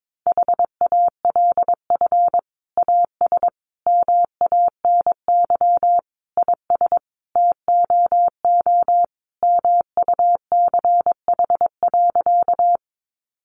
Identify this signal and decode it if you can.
Morse code, heard as a keyed tone, about 22 wpm, 700 Hz. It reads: HALF AS MANY IS TOO MUCH.
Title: Sentences from Top 400 Words 22wpm